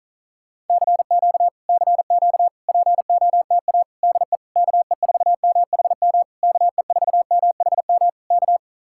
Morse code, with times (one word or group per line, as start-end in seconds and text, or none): 0.7-1.5 CQ
1.7-2.5 CQ
2.7-3.8 POTA
4.0-4.4 DE
4.6-6.2 KE4MHM
6.4-8.1 KE4MHM
8.3-8.6 K